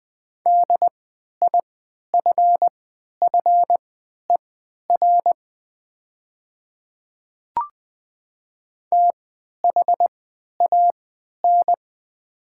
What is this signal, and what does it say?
Morse code, heard as a keyed tone, about 20 wpm, spaced as for 11 wpm, 690 Hz. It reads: DIFFER THAN
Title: 3x Character Spacing - Top 200 Words - 20wpm